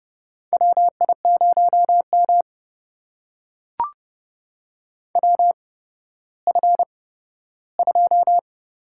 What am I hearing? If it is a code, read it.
Morse code, heard as a keyed tone, about 28 wpm, spaced as for 14 wpm, 700 Hz. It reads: WI0M WF2
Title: USA Call Signs - 4 Characters - 8x Character Spacing - 30wpm